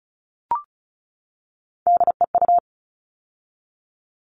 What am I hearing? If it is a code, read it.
Morse code, heard as a keyed tone, about 34 wpm, 700 Hz. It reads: DEU
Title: All Cut Numbers - 3-Digit 35wpm